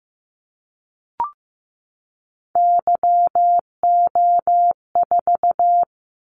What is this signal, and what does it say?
Morse code, heard as a keyed tone, about 15 wpm, 700 Hz. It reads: YO4
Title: Callsign Prefixes for Top 25 DXCC - 15wpm